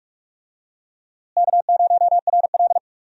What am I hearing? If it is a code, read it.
Morse code, heard as a keyed tone, about 43 wpm, 700 Hz. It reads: K0RL
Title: USA Call Signs - 4 Characters - Repeated 2x - 45wpm